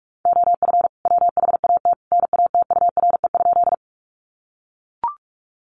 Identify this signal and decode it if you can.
Morse code, heard as a keyed tone, about 44 wpm, 700 Hz. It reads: OF WHAT NATURE?
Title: Sentences from Top 600 Words 45wpm